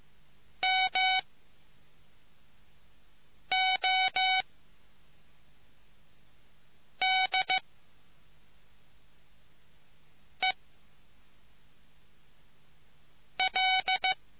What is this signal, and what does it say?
Morse code, heard as a keyed tone, about 15 wpm, spaced as for 3 wpm, 750 Hz. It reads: MODEL